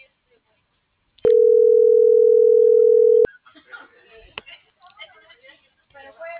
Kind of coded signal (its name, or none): telephone call progress tone